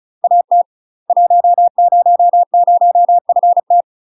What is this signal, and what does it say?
Morse code, heard as a keyed tone, about 35 wpm, 700 Hz. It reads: AT 100FT
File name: UK MASTER QSOs TRAINING / 35WPM / UK MASTER QSOs 0001